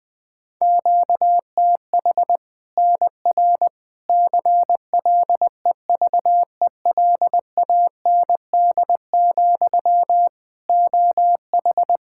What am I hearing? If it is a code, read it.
Morse code, heard as a keyed tone, about 20 wpm, 700 Hz. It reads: QTH NR CLEVELAND, OH
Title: K1USN SST QSO Practice 20wpm